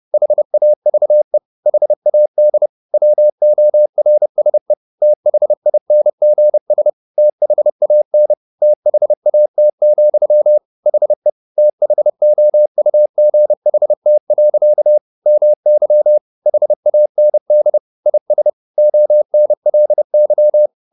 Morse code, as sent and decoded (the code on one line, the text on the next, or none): .... .- ...- . / .... .- -.. / .-- --- .-. ... / - .... .. -. --. ... / - .... .- -. / - .... .- - --..-- / .... / - .... --- ..- --. .... - .-.-.- / -- -.-- / .... .- -. -.. / .. ... / --- -. .-.. -.--
HAVE HAD WORSE THINGS THAN THAT, HE THOUGHT. MY HAND IS ONLY